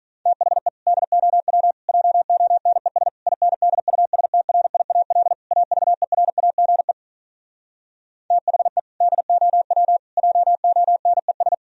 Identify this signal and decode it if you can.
Morse code, heard as a keyed tone, about 46 wpm, 700 Hz. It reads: THE DOW JONES INDUSTRIAL AVERAGE THE DOW JONES